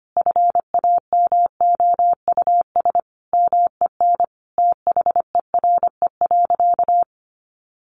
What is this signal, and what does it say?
Morse code, heard as a keyed tone, about 25 wpm, 700 Hz. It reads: FAMOUS MEN THERE.
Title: Sentences from Top 1500 Words 25wpm